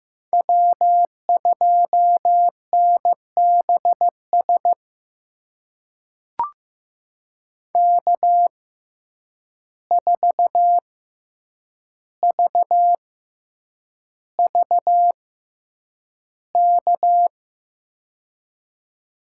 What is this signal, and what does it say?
Morse code, heard as a keyed tone, about 15 wpm, spaced as for 8 wpm, 700 Hz. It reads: W2NBS K4VVK
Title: USA Call Signs - 5 Characters - 6x Character Spacing - 15wpm